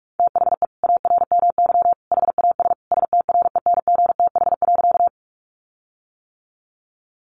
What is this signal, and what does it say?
Morse code, heard as a keyed tone, about 44 wpm, 700 Hz. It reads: THE ARMY HAS STRENGTH.
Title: Sentences from Top 1300 Words 45wpm